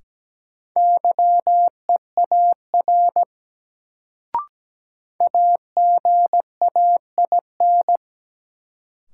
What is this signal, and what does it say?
Morse code, heard as a keyed tone, about 17 wpm, 700 Hz. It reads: YEAR AGAIN
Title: Sentences from Top 200 Words - New Word Drill - 17wpm